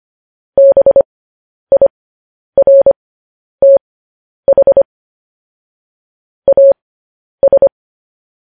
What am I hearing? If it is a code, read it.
Morse code, heard as a keyed tone, about 25 wpm, spaced as for 10 wpm, 550 Hz. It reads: BIRTH AS
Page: news-headlines-in-morse-code-at-10-wpm